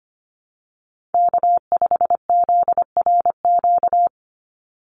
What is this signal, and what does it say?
Morse code, heard as a keyed tone, about 25 wpm, 700 Hz. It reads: K5ZRQ
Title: USA Call Signs 25wpm